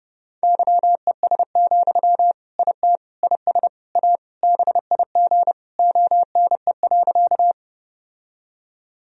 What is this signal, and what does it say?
Morse code, heard as a keyed tone, about 30 wpm, 700 Hz. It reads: YES, IT IS A BIG ONE.
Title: Sentences from Top 1000 Words - Review Part 1 - 30wpm